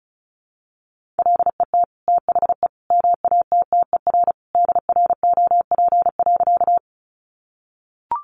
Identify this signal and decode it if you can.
Morse code, heard as a keyed tone, about 34 wpm, 700 Hz. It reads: LET THE MATTER DROP.
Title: Sentences from Top 1400 Words 35wpm